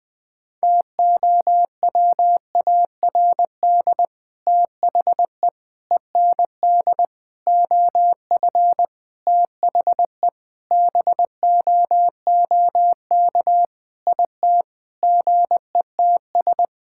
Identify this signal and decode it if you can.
Morse code, heard as a keyed tone, about 20 wpm, 700 Hz. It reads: TOWARD THE END OF THE BOOK IT GETS
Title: Sentences with 5-Word Phrases - 20wpm